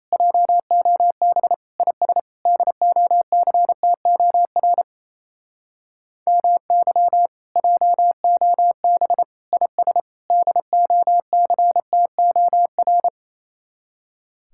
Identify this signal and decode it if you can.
Morse code, heard as a keyed tone, about 36 wpm, 700 Hz. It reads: JOB IS DOCTOR MY JOB IS DOCTOR